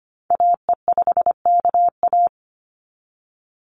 Morse code, heard as a keyed tone, about 25 wpm, 700 Hz. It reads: AE5KA
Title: NA SOTA Activator Callsigns 25wpm